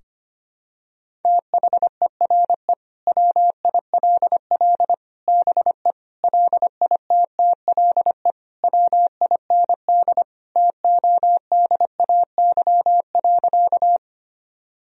Morse code, heard as a keyed tone, about 25 wpm, 700 Hz. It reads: THERE WILL BE LITTLE WIND TODAY.
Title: Sentences from Top 1000 Words - Review Part 2 - 25wpm